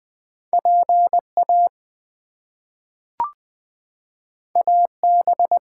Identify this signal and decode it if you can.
Morse code, heard as a keyed tone, about 20 wpm, 700 Hz. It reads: PA AB
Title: US States and Canadian Provinces 20wpm